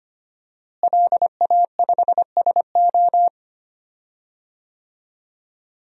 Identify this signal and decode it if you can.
Morse code, heard as a keyed tone, about 25 wpm, 700 Hz. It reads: LA5SO
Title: DXCC-100 Call Signs - 5 Characters - Repeated 2x - 25wpm